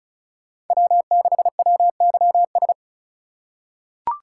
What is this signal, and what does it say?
Morse code, heard as a keyed tone, about 36 wpm, 700 Hz. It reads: W6WYS
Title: USA Call Signs - 3x Character Spacing - 35wpm